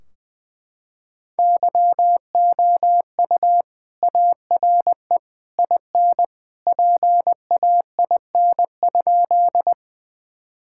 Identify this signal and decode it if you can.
Morse code, heard as a keyed tone, about 21 wpm, 700 Hz. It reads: YOU ARE IN PAIN?
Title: Sentences from Top 1600 Words 20wpm